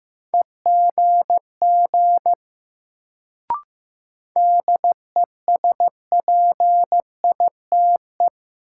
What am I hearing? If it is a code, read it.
Morse code, heard as a keyed tone, about 15 wpm, 700 Hz. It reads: EGG DESPITE
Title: Sentences from Top 1000 Words - Review Part 7 - New Word Drill - 15wpm